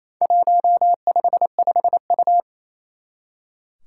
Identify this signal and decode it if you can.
Morse code, heard as a keyed tone, about 27 wpm, 700 Hz. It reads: 155U
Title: Four Challenging Letter-Number Combinations 28wpm